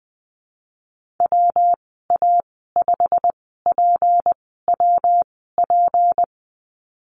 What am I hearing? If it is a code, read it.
Morse code, heard as a keyed tone, about 20 wpm, spaced as for 15 wpm, 700 Hz. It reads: WA5PWP
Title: 161 Lesson 26g - Letter g - Call Signs 20wpm